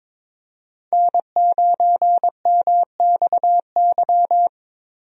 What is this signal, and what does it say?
Morse code, heard as a keyed tone, about 22 wpm, 700 Hz. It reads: N9MXY
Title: USA Call Signs - 5 Characters - Repeated 3x - 22wpm